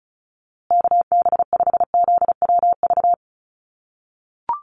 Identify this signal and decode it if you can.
Morse code, heard as a keyed tone, about 34 wpm, 700 Hz. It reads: KB5ZWV
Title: USA Call Signs Repeated 3x 35wpm